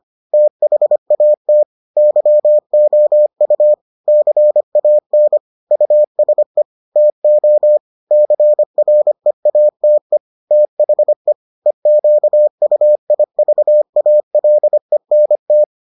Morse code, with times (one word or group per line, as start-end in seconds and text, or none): 0.3-1.6 THAT
2.0-3.7 YOU
4.1-5.4 CAN
5.7-6.6 USE
7.0-7.8 TO
8.1-10.2 CREATE
10.5-11.3 THE
11.7-15.6 EQUIVALENT